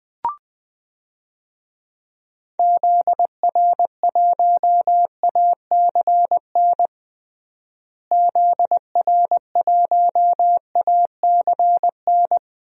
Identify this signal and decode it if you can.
Morse code, heard as a keyed tone, about 20 wpm, 700 Hz. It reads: ZR1ACN ZR1ACN
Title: DXCC-100 Call Signs - 6 Characters - Repeated 3x - 20wpm